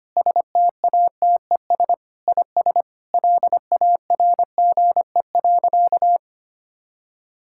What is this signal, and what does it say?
Morse code, heard as a keyed tone, about 25 wpm, 700 Hz. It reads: STATES IS LARGE.
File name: practice / 25wpm / Sentences with 4-Word Phrases - 25wpm